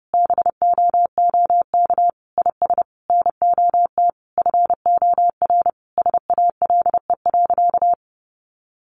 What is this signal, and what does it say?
Morse code, heard as a keyed tone, about 30 wpm, 700 Hz. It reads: BOOK IS NOT FOR SALE.
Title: Sentences from Top 1400 Words 30wpm